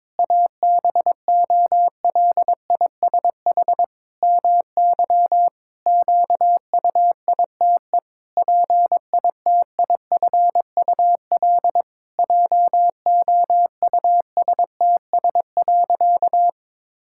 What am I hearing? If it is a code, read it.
Morse code, heard as a keyed tone, about 22 wpm, 700 Hz. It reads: ABOLISH MY QUITE PITIFUL JOUSTS.
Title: Pangram Challenge v2 - 22wpm